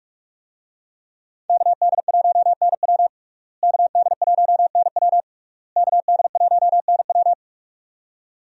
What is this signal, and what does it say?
Morse code, heard as a keyed tone, about 46 wpm, 700 Hz. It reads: KD1NW KD1NW KD1NW
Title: USA Call Signs - 5 Characters - Repeated 3x - 45wpm